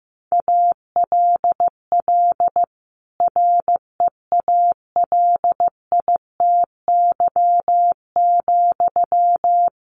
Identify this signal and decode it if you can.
Morse code, heard as a keyed tone, about 15 wpm, 700 Hz. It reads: ALL REALITY,
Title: Sentences from Top 1400 Words 15wpm